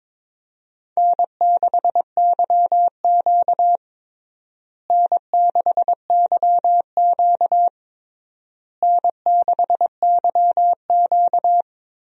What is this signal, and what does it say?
Morse code, heard as a keyed tone, about 22 wpm, 700 Hz. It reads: N6YQ N6YQ N6YQ
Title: USA Call Signs - 4 Characters - Repeated 3x - 22wpm